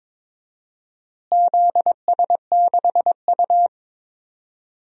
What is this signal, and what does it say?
Morse code, heard as a keyed tone, about 22 wpm, 700 Hz. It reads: ZS6U
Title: DXCC-100 Call Signs - 4 Characters - Repeated 3x - 22wpm